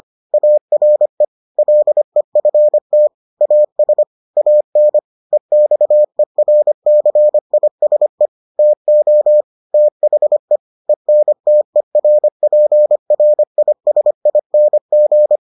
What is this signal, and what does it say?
Morse code, heard as a keyed tone, about 25 wpm, 600 Hz. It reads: ARE LEFT AS AN EXERCISE TO THE ENTERPRISING